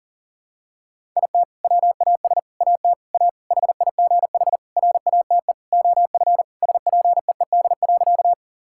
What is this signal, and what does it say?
Morse code, heard as a keyed tone, about 40 wpm, 700 Hz. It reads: IT WAS AT A HIGH RATE OF SPEED.
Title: Sentences from Top 1000 Words - Review Part 5 - 40wpm